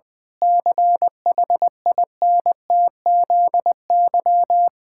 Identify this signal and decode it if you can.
Morse code, heard as a keyed tone, about 20 wpm, 700 Hz. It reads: CHINTZY
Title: Pangram Challenge v2 - 20wpm